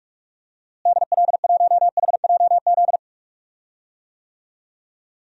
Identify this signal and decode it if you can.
Morse code, heard as a keyed tone, about 46 wpm, 700 Hz. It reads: DL1HJZ
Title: DXCC-100 Call Signs - 6 Characters - Repeated 3x - 45wpm